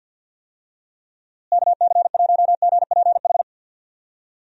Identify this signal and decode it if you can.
Morse code, heard as a keyed tone, about 48 wpm, 700 Hz. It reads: KC1GPH
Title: USA Call Signs - 6 Characters - Repeated 3x - 50wpm